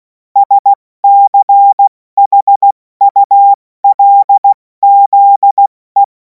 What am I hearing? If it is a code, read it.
Morse code, heard as a keyed tone, about 16 wpm, 800 Hz. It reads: SCHULZE
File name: Morsekurs / Witze / Witz 26. Mai 2026 70 F80